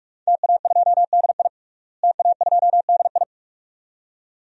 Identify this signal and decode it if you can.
Morse code, heard as a keyed tone, about 46 wpm, 700 Hz. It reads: TA2DI TA2DI
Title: DXCC-100 Call Signs - 5 Characters - Repeated 2x - 45wpm